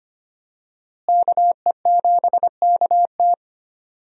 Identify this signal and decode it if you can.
Morse code, heard as a keyed tone, about 25 wpm, 700 Hz. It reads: KE7KT
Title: USA Call Signs - 5 Characters - Repeated 3x - 25wpm